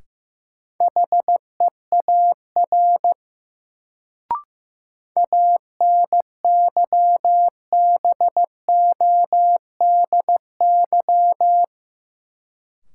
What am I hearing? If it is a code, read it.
Morse code, heard as a keyed tone, about 15 wpm, 700 Hz. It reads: HEAR ANYBODY
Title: Sets of 5 Words - New Word Drill - 15wpm